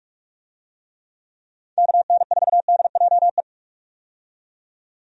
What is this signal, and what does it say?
Morse code, heard as a keyed tone, about 46 wpm, 700 Hz. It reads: KN4DJE
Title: USA Call Signs - 6 Characters - 45wpm